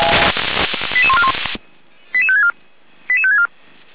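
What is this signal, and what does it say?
Five-tone selective call (ZVEI), heard as quick runs of five tones, tones of about 70 ms. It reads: 88141 88564 88564
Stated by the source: Received on an SDR; ZVEI five-tone call 88147.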